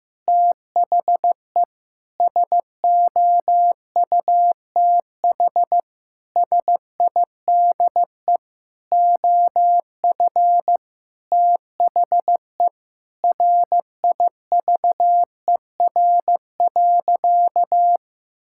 Morse code, heard as a keyed tone, about 15 wpm, 700 Hz. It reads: THE SOUTH SIDE OF THE RIVER.
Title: Sentences from Top 1000 Words - Review Part 5 - 15wpm